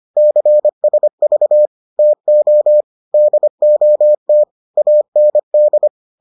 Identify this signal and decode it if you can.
Morse code, heard as a keyed tone, about 25 wpm, 600 Hz. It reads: CSV TO DOT AND